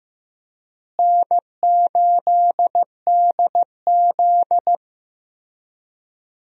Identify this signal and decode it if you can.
Morse code, heard as a keyed tone, about 15 wpm, 700 Hz. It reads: N8DZ
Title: USA Call Signs - 4 Characters - Repeated 3x - 15wpm